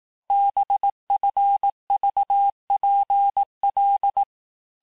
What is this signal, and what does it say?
Morse code, heard as a keyed tone, about 18 wpm, 800 Hz. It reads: BFVPL
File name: cours-du-lundi files / cours-1852-texte-chiffre-vitesse-17MPM